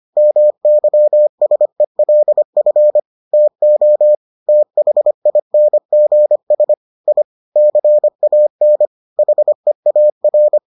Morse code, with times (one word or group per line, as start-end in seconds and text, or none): 0.2-3.0 MYSELF
3.3-4.2 TO
4.5-6.7 THINGS
7.1-7.2 I
7.6-8.9 CAN
9.2-10.6 HEAR